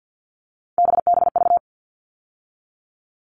Morse code, heard as a keyed tone, about 50 wpm, 700 Hz. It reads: BBV
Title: All Cut Numbers - 3-Digit 50wpm